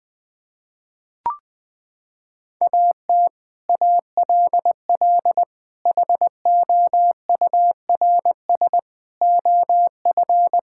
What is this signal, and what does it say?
Morse code, heard as a keyed tone, about 20 wpm, 700 Hz. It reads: AT ALL HOURS OF
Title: Sets of 8 Words - 20wpm